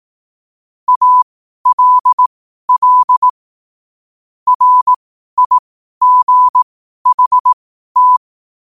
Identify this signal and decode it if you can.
Morse code, heard as a keyed tone, about 18 wpm, spaced as for 12 wpm, 1000 Hz. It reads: ALL RIGHT
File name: qotdfiles / 12wpm